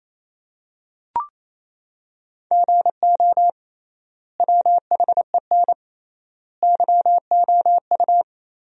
Morse code, heard as a keyed tone, about 28 wpm, 700 Hz. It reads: GO WHEN YOU
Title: 3x Word-Spacing - Sentences from Top 500 Words - 28wpm